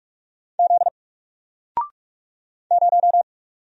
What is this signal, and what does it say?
Morse code, heard as a keyed tone, about 44 wpm, 700 Hz. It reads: Z 0
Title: Single Letter-Number Alt 45wpm